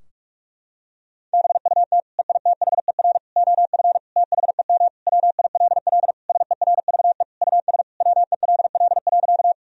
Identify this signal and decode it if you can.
Morse code, heard as a keyed tone, about 46 wpm, 700 Hz. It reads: BUT EITHER OF THEM WILL SERVE US WELL.
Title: Sentences from Top 1000 Words - Review Part 4 - 45wpm